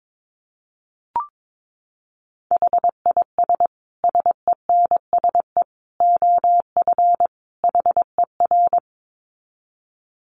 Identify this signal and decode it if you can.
Morse code, heard as a keyed tone, about 22 wpm, 700 Hz. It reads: HIS SENSE OF HER
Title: Dit Challenge 22wpm